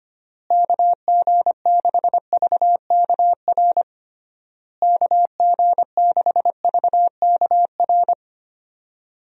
Morse code, heard as a keyed tone, about 25 wpm, 700 Hz. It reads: KG6VKR KG6VKR
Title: USA Call Signs - 6 Characters - Repeated 3x - 25wpm